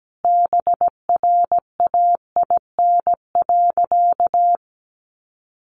Morse code, heard as a keyed tone, about 17 wpm, 700 Hz. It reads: BRAIN.
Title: A Sound Of Thunder 17wpm